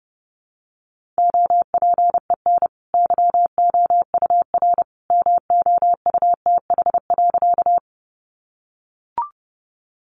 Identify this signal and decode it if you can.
Morse code, heard as a keyed tone, about 30 wpm, 700 Hz. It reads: OPEN YOUR MOUTH.
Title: Sentences from Top 1500 Words 30wpm